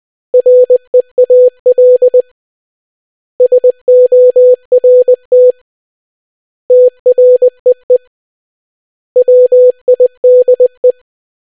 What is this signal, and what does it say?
Morse code, heard as a keyed tone, about 20 wpm, 500 Hz. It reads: REAL SORT TREE WIDE